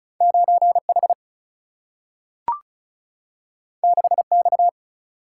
Morse code, heard as a keyed tone, about 36 wpm, 700 Hz. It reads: 9 H 6 X
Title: Single-Letter Single-Number 35wpm